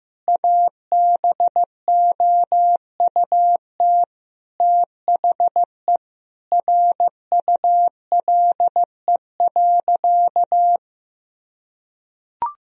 Morse code, heard as a keyed tone, about 15 wpm, 700 Hz. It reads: ABOUT THE RULE.
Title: Sentences from Top 1000 Words - Review Part 2 - 15wpm